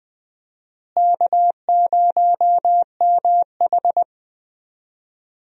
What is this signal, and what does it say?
Morse code, heard as a keyed tone, about 20 wpm, 700 Hz. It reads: K0MH
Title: USA Call Signs - 4 Characters - Repeated 3x - 20wpm